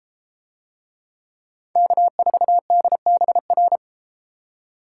Morse code, heard as a keyed tone, about 32 wpm, 700 Hz. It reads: K4DBR